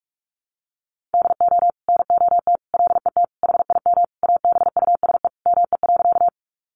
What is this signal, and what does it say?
Morse code, heard as a keyed tone, about 44 wpm, 700 Hz. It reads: DO NOT LET HIM ABUSE ME.
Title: Sentences from Top 1300 Words 45wpm